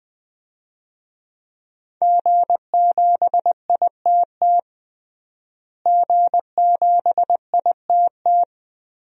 Morse code, heard as a keyed tone, about 20 wpm, 700 Hz. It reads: G7ITT G7ITT
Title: DXCC-100 Call Signs - 5 Characters - Repeated 2x - 20wpm